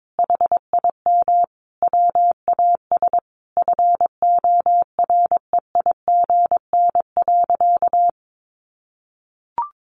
Morse code, heard as a keyed tone, about 22 wpm, 700 Hz. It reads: HIM WAS FOREIGN.
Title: Sentences from Top 1400 Words 22wpm